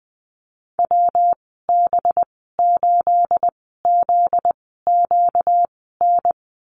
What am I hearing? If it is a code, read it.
Morse code, heard as a keyed tone, about 20 wpm, spaced as for 14 wpm, 700 Hz. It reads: WB8ZQN
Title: 245 Lesson 39g - Letter z - Call Signs 20wpm